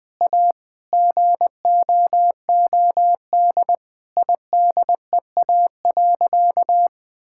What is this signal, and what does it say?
Morse code, heard as a keyed tone, about 20 wpm, 700 Hz. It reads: A GOOD IDEA.
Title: Sentences with 3-Word Phrases - 20wpm